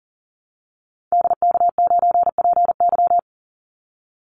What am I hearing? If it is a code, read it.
Morse code, heard as a keyed tone, about 40 wpm, 700 Hz. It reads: DK9PY
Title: Call Signs Repeated 2x - 40wpm